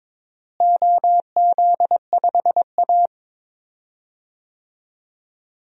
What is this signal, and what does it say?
Morse code, heard as a keyed tone, about 22 wpm, 700 Hz. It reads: OZ5A
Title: DXCC-100 Call Signs - 4 Characters - Repeated 3x - 22wpm